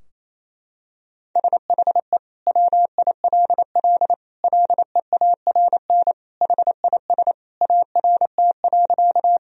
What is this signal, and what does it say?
Morse code, heard as a keyed tone, about 28 wpm, 700 Hz. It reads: SHE WILL LEARN HIS ART.
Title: Sentences from Top 1000 Words - Review Part 3 - 28wpm